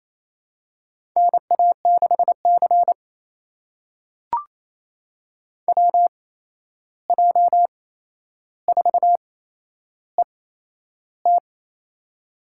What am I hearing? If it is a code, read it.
Morse code, heard as a keyed tone, about 28 wpm, spaced as for 11 wpm, 700 Hz. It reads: NA6C WJ4ET
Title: USA Call Signs - 8x Character Spacing - 28wpm